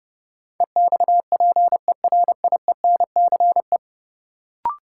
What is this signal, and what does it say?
Morse code, heard as a keyed tone, about 30 wpm, 700 Hz. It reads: EXPERIENCE
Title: Sentences from Top 600 Words - New Word Drill - 30wpm